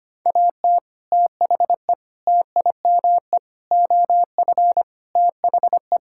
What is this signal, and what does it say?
Morse code, heard as a keyed tone, about 25 wpm, 700 Hz. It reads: AT THE TIME OF THE
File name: practice / 25wpm / Sentences with 4-Word Phrases - 25wpm